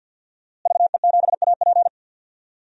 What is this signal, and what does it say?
Morse code, heard as a keyed tone, about 51 wpm, 700 Hz. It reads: VE7AP
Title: DXCC-100 Call Signs - 5 Characters - Repeated 3x - 50wpm